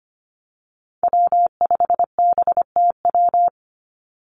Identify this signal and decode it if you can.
Morse code, heard as a keyed tone, about 25 wpm, 700 Hz. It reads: W5BTW
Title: USA Call Signs 25wpm